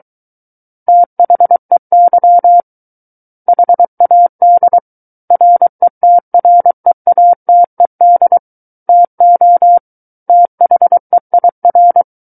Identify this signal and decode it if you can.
Morse code, heard as a keyed tone, about 23 wpm, 700 Hz. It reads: THEY HAD RETREATED TO THEIR
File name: Book One / 14-Chapter 14 - 23 WPM